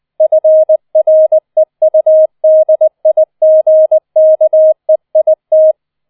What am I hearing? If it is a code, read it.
Morse code, heard as a keyed tone, about 20 wpm, 630 Hz. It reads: FREUDIGKEIT
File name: 20wpm